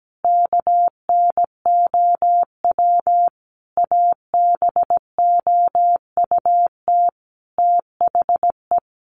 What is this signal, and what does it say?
Morse code, heard as a keyed tone, about 17 wpm, 700 Hz. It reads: KNOW ABOUT THE
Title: Sentences from Top 1400 Words 17wpm